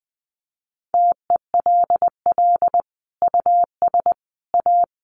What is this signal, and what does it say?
Morse code, heard as a keyed tone, about 19 wpm, 700 Hz. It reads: TELL US A
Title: Sets of 5 Words Encore2 20wpm